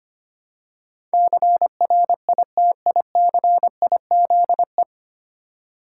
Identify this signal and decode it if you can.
Morse code, heard as a keyed tone, about 25 wpm, 700 Hz. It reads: CRITICIZE
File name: practice / 25wpm / 9 Letter Words - First 3 Letters Given - 25wpm